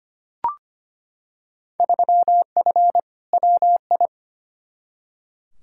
Morse code, heard as a keyed tone, about 25 wpm, 700 Hz. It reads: 3F WI
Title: Field Day Exchanges - 25wpm